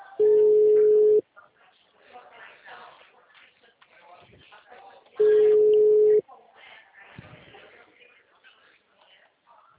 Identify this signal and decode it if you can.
Ringback tone (Europe, 425 Hz), with speech.